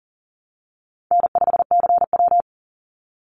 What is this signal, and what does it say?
Morse code, heard as a keyed tone, about 40 wpm, 700 Hz. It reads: N5CW
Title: Easy Call Signs Repeated 2x - 40wpm